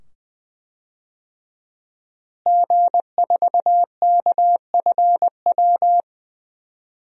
Morse code, heard as a keyed tone, about 20 wpm, 700 Hz. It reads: G4KFW
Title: DXCC-100 Call Signs - 5 Characters - Repeated 2x - 20wpm